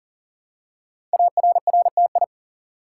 Morse code, heard as a keyed tone, about 39 wpm, 700 Hz. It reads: ARRTI
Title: Code Groups - 5 Letters - 40wpm